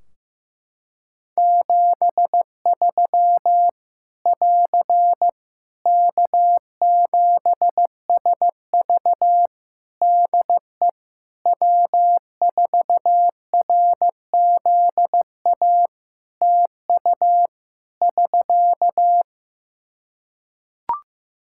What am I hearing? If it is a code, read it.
Morse code, heard as a keyed tone, about 15 wpm, 700 Hz. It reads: 73 <AR> K7SV DE W4RZA TU <SK>
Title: Practice QSO - Exchange 4 - Ending - 15wpm